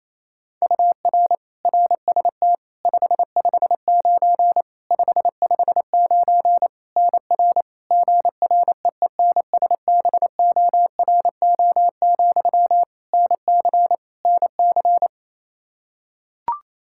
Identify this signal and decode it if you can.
Morse code, heard as a keyed tone, about 28 wpm, 700 Hz. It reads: UR RST 559 559 NR GREENSBORO, NC NC
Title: K1USN SST QSO Practice 28wpm